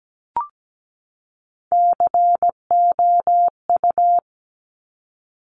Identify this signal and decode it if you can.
Morse code, heard as a keyed tone, about 17 wpm, 700 Hz. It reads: COU
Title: Top Three Letter Combinations in Words 17wpm